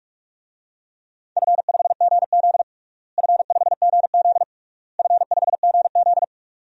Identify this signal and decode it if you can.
Morse code, heard as a keyed tone, about 46 wpm, 700 Hz. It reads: F5GZ F5GZ F5GZ
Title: DXCC-100 Call Signs - 4 Characters - Repeated 3x - 45wpm